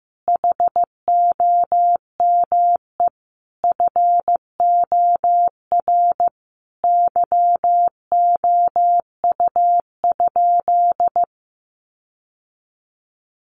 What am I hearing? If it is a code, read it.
Morse code, heard as a keyed tone, about 15 wpm, 700 Hz. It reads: HOME FOR YOU?
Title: Sentences from Top 1400 Words 15wpm